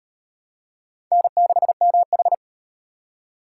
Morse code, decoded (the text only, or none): N6MH